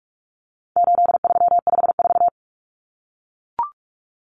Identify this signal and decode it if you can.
Morse code, heard as a keyed tone, about 44 wpm, 700 Hz. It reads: 8354
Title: Numbers - 4-Digit 45wpm